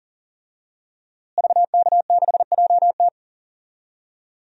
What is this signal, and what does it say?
Morse code, heard as a keyed tone, about 38 wpm, 700 Hz. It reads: VK6JT
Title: DXCC-100 Call Signs - 5 Characters - Repeated 2x - 40wpm